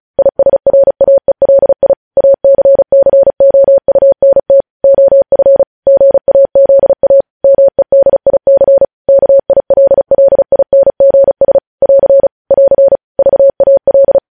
Morse code, heard as a keyed tone, about 36 wpm, 550 Hz. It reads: ISRAELI ACCOUNT OF GAZA MEDIC KILLINGS <AR> <AR> VAL
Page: news headlines in morse code at 35 wpm